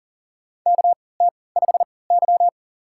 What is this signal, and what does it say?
Morse code, heard as a keyed tone, about 40 wpm, spaced as for 23 wpm, 700 Hz. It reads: KT5Y
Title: USA Call Signs - 4 Characters - 3x Character Spacing - 40wpm